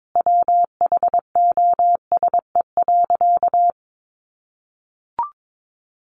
Morse code, heard as a keyed tone, about 22 wpm, 700 Hz. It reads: WHOSE.
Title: Sentences from Top 1200 Words 22wpm